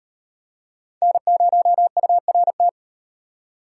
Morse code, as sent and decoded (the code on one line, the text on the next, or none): -. ----- ..- .-. -
N0URT